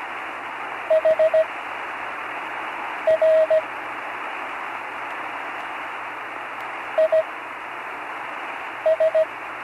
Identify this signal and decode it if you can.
Morse code, heard as a keyed tone, about 17 wpm, spaced as for 5 wpm, 640 Hz, with radio noise. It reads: HR IS
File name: CW Learning Programs / Learning Morse Code (from Tune in the World) / 43 on the air